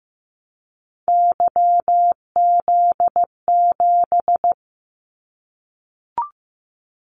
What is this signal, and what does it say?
Morse code, heard as a keyed tone, about 15 wpm, 700 Hz. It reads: YZ7
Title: Callsign Prefixes for Top 25 DXCC - Repeated 3x - 15wpm